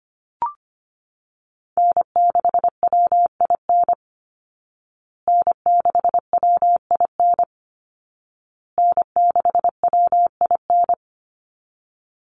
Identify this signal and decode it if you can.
Morse code, heard as a keyed tone, about 25 wpm, 700 Hz. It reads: N6WIN N6WIN N6WIN
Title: Call Signs Repeated 2x - 25wpm